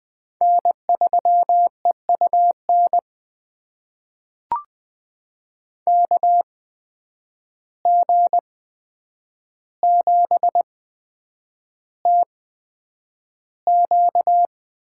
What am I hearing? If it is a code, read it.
Morse code, heard as a keyed tone, about 20 wpm, spaced as for 9 wpm, 700 Hz. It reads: N3EUN KG7TQ
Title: USA Call Signs - 8x Character Spacing - 20wpm